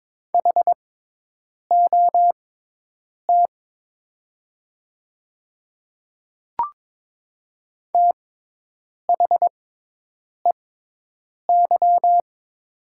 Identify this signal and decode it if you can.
Morse code, heard as a keyed tone, about 22 wpm, spaced as for 8 wpm, 700 Hz. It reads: HOT THEY
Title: 6x Character Spacing - Top 100 Words - 22wpm